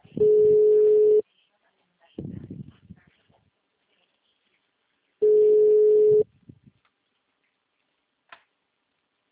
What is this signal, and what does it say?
Ringback tone (Europe, 425 Hz), with speech.
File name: out- -2015- - - .31356